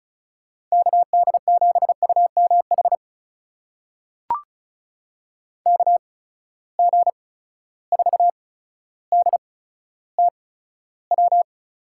Morse code, heard as a keyed tone, about 36 wpm, spaced as for 15 wpm, 700 Hz. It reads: KD7UMH KG4DTW
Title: USA Call Signs - 6 Characters - 8x Character Spacing - 35wpm